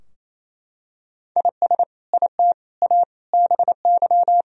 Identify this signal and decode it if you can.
Morse code, heard as a keyed tone, about 28 wpm, 700 Hz. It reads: IS IT A BY
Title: Sentences from Top 1000 Words - Review Part 2 - 28wpm